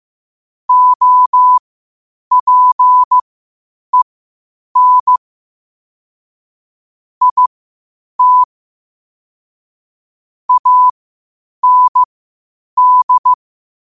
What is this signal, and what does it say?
Morse code, heard as a keyed tone, about 15 wpm, spaced as for 8 wpm, 1000 Hz. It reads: OPEN IT AND
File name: qotdfiles / 8wpm